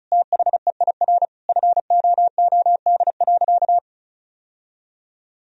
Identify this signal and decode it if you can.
Morse code, heard as a keyed tone, about 36 wpm, 700 Hz. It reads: THEIR FOOD.
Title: Sentences from Top 1000 Words - Review Part 1 - 35wpm